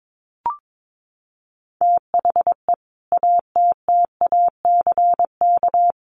Morse code, heard as a keyed tone, about 22 wpm, 700 Hz. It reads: THE ATTACK
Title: Sentences from Top 1500 Words 22wpm